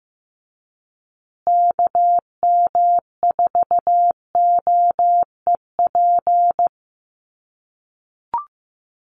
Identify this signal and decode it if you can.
Morse code, heard as a keyed tone, about 15 wpm, 700 Hz. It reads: KM4OEP
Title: USA Call Signs Repeated 3x 15wpm